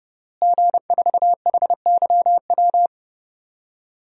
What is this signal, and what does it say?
Morse code, heard as a keyed tone, about 32 wpm, 700 Hz. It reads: G4HYW
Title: DXCC-100 Call Signs - 5 Characters - Repeated 3x - 30wpm